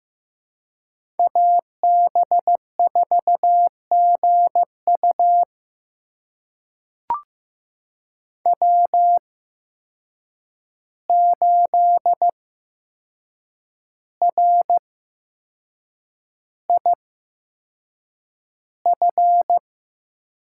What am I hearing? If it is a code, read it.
Morse code, heard as a keyed tone, about 15 wpm, spaced as for 6 wpm, 700 Hz. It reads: AB4GU W8RIF
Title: USA Call Signs - 5 Characters - 8x Character Spacing - 15wpm